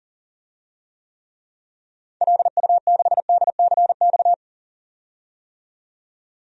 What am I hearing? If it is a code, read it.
Morse code, heard as a keyed tone, about 37 wpm, 700 Hz. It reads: LU6DCX
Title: DXCC-100 Call Signs - 6 Characters - 40wpm